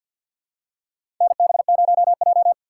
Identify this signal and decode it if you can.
Morse code, heard as a keyed tone, about 46 wpm, 700 Hz. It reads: NB0J
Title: USA Call Signs - 4 Characters - 2x Character Spacing - 50wpm